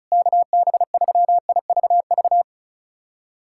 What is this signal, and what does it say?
Morse code, heard as a keyed tone, about 34 wpm, 700 Hz. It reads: KB3IVV
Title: USA Callsign DE Callsign - 6 Characters - 35wpm